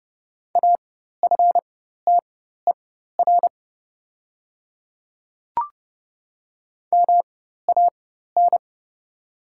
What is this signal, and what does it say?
Morse code, heard as a keyed tone, about 29 wpm, spaced as for 14 wpm, 700 Hz. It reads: AFTER MAN
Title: 4x Character Spacing - Top 300 Words - 30wpm